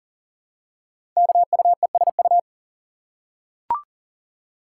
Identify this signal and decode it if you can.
Morse code, heard as a keyed tone, about 39 wpm, 700 Hz. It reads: KUESU
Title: Code Groups - 5 Letter:Numbers - 40wpm